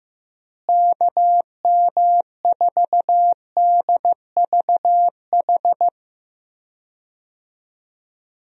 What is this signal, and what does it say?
Morse code, heard as a keyed tone, about 15 wpm, 700 Hz. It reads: KM4DVH
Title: USA Call Signs - 6 Characters - Repeated 3x - 15wpm